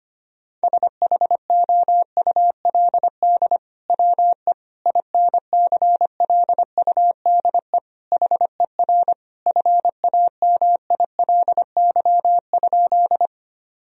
Morse code, heard as a keyed tone, about 25 wpm, 700 Hz. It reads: SHOULD WE INCLUDE HER FAMILY?
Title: Sentences from Top 1000 Words - Review Part 3 - 25wpm